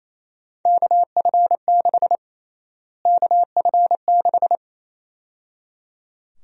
Morse code, heard as a keyed tone, about 28 wpm, 700 Hz. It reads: KF6 KF6
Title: USA Call Sign Prefixes - Repeated 2x - 28wpm